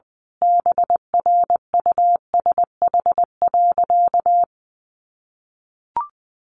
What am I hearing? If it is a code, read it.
Morse code, heard as a keyed tone, about 20 wpm, 700 Hz. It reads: BRUSH.
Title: Sentences from Top 1100 Words 20wpm